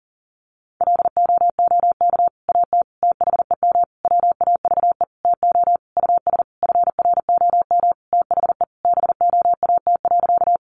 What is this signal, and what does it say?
Morse code, heard as a keyed tone, about 40 wpm, 700 Hz. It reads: LOOK AT THEM WAVE TO US FROM THE BOAT.
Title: Sentences from Top 1400 Words 40wpm